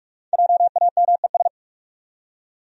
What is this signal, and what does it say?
Morse code, heard as a keyed tone, about 44 wpm, 700 Hz. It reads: JAMES
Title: USA Common Names - Max 5 Letters - 45wpm